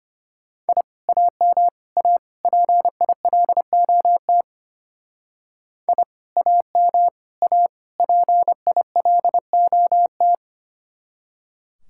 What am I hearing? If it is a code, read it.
Morse code, heard as a keyed tone, about 31 wpm, 700 Hz. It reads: I AM A PILOT I AM A PILOT